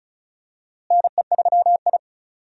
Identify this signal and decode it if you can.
Morse code, heard as a keyed tone, about 36 wpm, 700 Hz. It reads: NE3I
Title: USA Call Signs - 4 Characters - Repeated 3x - 35wpm